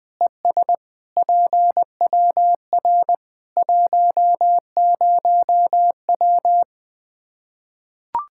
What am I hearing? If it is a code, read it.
Morse code, heard as a keyed tone, about 20 wpm, 700 Hz. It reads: ES PWR 10W
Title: K1USN SST QSO Practice 20wpm